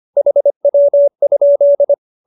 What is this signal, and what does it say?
Morse code, heard as a keyed tone, about 25 wpm, 570 Hz. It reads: HW?
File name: QSO305 25